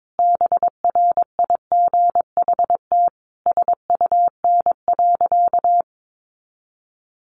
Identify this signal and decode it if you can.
Morse code, heard as a keyed tone, about 22 wpm, 700 Hz. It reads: BRIGHT SUN.
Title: Sentences from Top 1000 Words 22wpm